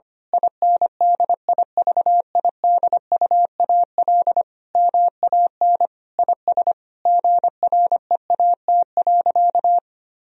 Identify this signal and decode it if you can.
Morse code, heard as a keyed tone, about 25 wpm, 700 Hz. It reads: INDIVIDUAL MAN IS GREAT.
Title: Sentences from Top 1000 Words - Review Part 4 - 25wpm